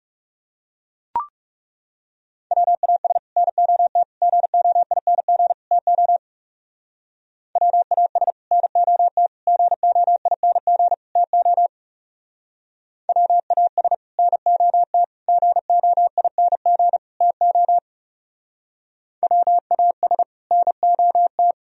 Morse code, heard as a keyed tone, about 38 wpm, 700 Hz. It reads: WAS NOT GOING TO WAS NOT GOING TO WAS NOT GOING TO WAS NOT